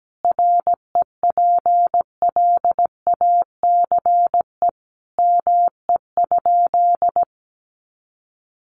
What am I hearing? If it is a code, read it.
Morse code, heard as a keyed tone, about 17 wpm, 700 Hz. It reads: REPLACE ME?
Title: Sentences from Top 1400 Words 17wpm